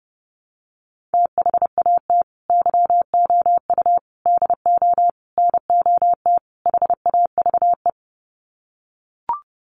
Morse code, heard as a keyed tone, about 28 wpm, 700 Hz. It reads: THAT YOU DO NOT HAVE
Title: Sets of 5 Words Encore 30wpm